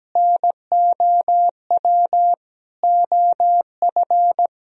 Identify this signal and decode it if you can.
Morse code, heard as a keyed tone, about 17 wpm, 700 Hz. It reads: NOW OF
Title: Sentences from Top 1000 Words - Review Part 3 - 17wpm